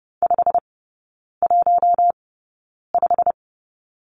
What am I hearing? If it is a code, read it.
Morse code, heard as a keyed tone, about 30 wpm, spaced as for 9 wpm, 700 Hz. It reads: 515
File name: course / 30:10wpm / 087 Lesson 15e - Number 5 - Three Characters 30wpm